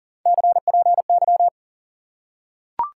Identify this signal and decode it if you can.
Morse code, heard as a keyed tone, about 40 wpm, 700 Hz. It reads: CPY
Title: Top 90 QSO Elements 40wpm